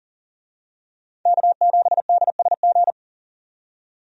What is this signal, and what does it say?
Morse code, heard as a keyed tone, about 40 wpm, 700 Hz. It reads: K7DSG
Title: USA Call Signs - 5 Characters - 40wpm